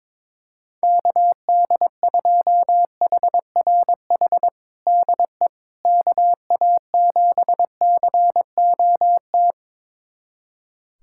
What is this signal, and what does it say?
Morse code, heard as a keyed tone, about 22 wpm, 700 Hz. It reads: KD2HRH DE KA7COT
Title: USA Callsign DE Callsign 22wpm